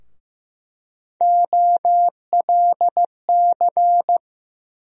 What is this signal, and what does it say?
Morse code, heard as a keyed tone, about 15 wpm, 700 Hz. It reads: OLC
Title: USA Call Sign Postfixes - 3 Letters 15wpm